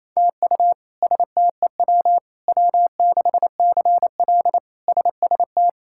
Morse code, heard as a keyed tone, about 28 wpm, 700 Hz. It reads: TU STEW W6CL SST
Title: K1USN SST QSO Practice - 28wpm